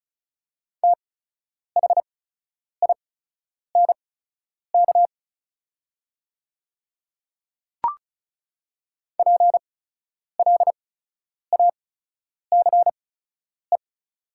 Morse code, heard as a keyed tone, about 36 wpm, spaced as for 10 wpm, 700 Hz. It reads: THINK PLACE
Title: 8x Character Spacing - Top 200 Words - 35wpm